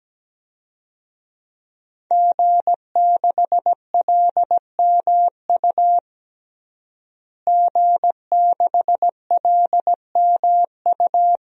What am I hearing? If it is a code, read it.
Morse code, heard as a keyed tone, about 17 wpm, 700 Hz. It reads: G6LMU G6LMU
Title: DXCC-100 Call Signs - 5 Characters - Repeated 3x - 17wpm